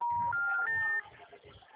Special information tone, with noise.
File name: out- -2010- - - .86685